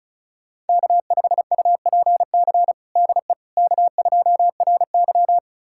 Morse code, heard as a keyed tone, about 36 wpm, 700 Hz. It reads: K5UPC DE K2RY
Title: USA Callsign DE Callsign 35wpm